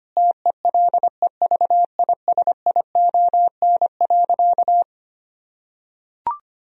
Morse code, heard as a keyed tone, about 25 wpm, 700 Hz. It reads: TELEVISION.
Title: Sentences from Top 1000 Words - Review Part 4 - 25wpm